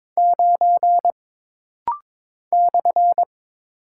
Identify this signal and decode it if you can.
Morse code, heard as a keyed tone, about 22 wpm, 700 Hz. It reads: 9 /
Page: This rapid-fire timing is geared to push your ICR (Instant Character Recognition) skills.